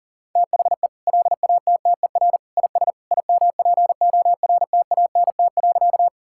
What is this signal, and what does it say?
Morse code, heard as a keyed tone, about 37 wpm, 700 Hz. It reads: THE LATTER IS IMPORTANT.
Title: Sentences from Top 1600 Words 40wpm